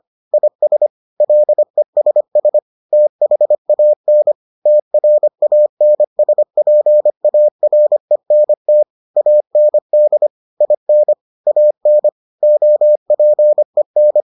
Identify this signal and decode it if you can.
Morse code, heard as a keyed tone, about 25 wpm, 600 Hz. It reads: IS LESS THAN TRANSPARENT AND IN AN OPEN